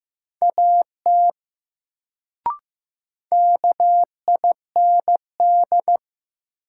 Morse code, heard as a keyed tone, about 15 wpm, 700 Hz. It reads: AT KIND
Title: Sets of 2 Words - New Word Drill - 15wpm